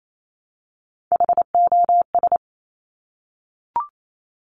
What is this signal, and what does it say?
Morse code, heard as a keyed tone, about 28 wpm, 700 Hz. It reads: HOS
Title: Top Three Letter Combinations in Words 28wpm